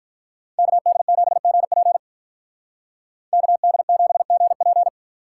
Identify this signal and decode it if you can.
Morse code, heard as a keyed tone, about 50 wpm, 700 Hz. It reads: KD7GP KD7GP